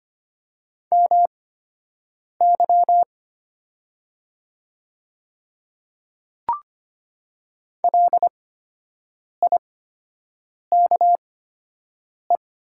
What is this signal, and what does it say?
Morse code, heard as a keyed tone, about 25 wpm, spaced as for 7 wpm, 700 Hz. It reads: MY LIKE
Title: 8x Character Spacing - Top 100 Words - 25wpm